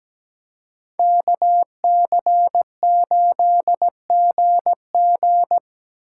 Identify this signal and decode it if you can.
Morse code, heard as a keyed tone, about 20 wpm, 700 Hz. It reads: KC8GG
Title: Speed Racing - USA 2x2 Call Signs 17wpm